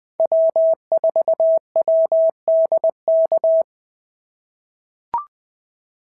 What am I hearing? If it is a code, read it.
Morse code, heard as a keyed tone, about 20 wpm, 690 Hz. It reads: W4WDK
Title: POTA QSO Practice - 20wpm